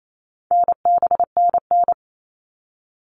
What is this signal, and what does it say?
Morse code, heard as a keyed tone, about 28 wpm, 700 Hz. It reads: NBNN
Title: All Cut Numbers - 4-Digit 28wpm